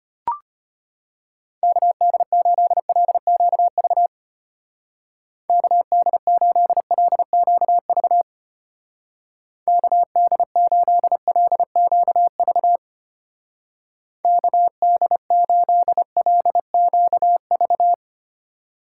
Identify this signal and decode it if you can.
Morse code, heard as a keyed tone, about 31 wpm, 700 Hz. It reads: KD8LQV KD8LQV KD8LQV KD8LQV
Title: Speed Racing - USA 2x3 Call Signs 25wpm